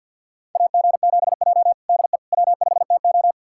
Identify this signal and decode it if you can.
Morse code, heard as a keyed tone, about 51 wpm, 700 Hz. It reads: AG7J DE W5TO
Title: USA Callsign DE Callsign - 4 Characters - 50wpm